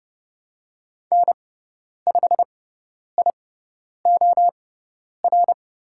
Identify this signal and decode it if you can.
Morse code, heard as a keyed tone, about 30 wpm, spaced as for 10 wpm, 690 Hz. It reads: N5IOR
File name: Lesson 17g - Letter u - Call Signs 30wpm